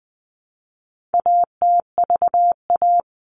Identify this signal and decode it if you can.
Morse code, heard as a keyed tone, about 20 wpm, 700 Hz. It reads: ATVA
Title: All Cut Numbers - 4-Digit 20wpm